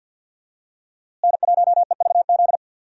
Morse code, heard as a keyed tone, about 47 wpm, 700 Hz. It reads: N1EVZ
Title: USA Call Signs - 5 Characters - Repeated 3x - 50wpm